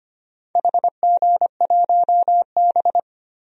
Morse code, heard as a keyed tone, about 25 wpm, 700 Hz. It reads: HG1B
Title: DXCC-100 Call Signs - 4 Characters - Repeated 3x - 25wpm